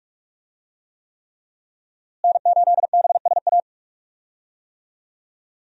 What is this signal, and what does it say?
Morse code, heard as a keyed tone, about 42 wpm, 700 Hz. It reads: N8BSA
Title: USA Call Signs - 5 Characters - Repeated 3x - 45wpm